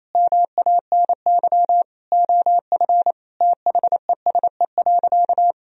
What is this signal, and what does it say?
Morse code, heard as a keyed tone, about 28 wpm, 700 Hz. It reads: MANY OF THESE.
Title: Sentences from Top 1000 Words - Review Part 1 - 28wpm